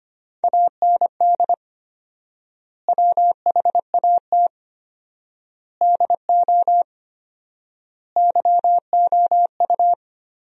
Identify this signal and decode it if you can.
Morse code, heard as a keyed tone, about 25 wpm, 700 Hz. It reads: AND WHAT DO YOU
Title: 4x Word-Spacing - Sentences from Top 100 Words - 25wpm